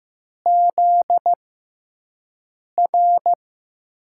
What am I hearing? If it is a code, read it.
Morse code, heard as a keyed tone, about 15 wpm, spaced as for 5 wpm, 700 Hz. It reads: ZR
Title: USA Call Signs - 5 Characters - 6x Character Spacing - 15wpm